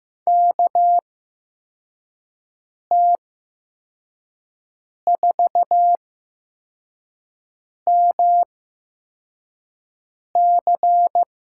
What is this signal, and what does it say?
Morse code, heard as a keyed tone, about 15 wpm, spaced as for 4 wpm, 700 Hz. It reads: KT4MC
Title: USA Call Signs - 5 Characters - 8x Character Spacing - 15wpm